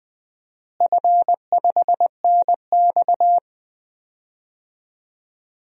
Morse code, heard as a keyed tone, about 20 wpm, 700 Hz. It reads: F5NX
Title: DXCC-100 Call Signs - 4 Characters - Repeated 2x - 20wpm